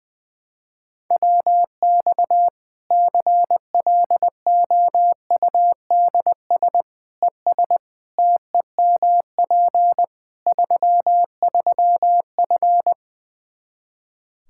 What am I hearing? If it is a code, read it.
Morse code, heard as a keyed tone, about 20 wpm, 700 Hz. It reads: WX CLOUDS ES TEMP 33F
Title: K1USN SST QSO Practice 20wpm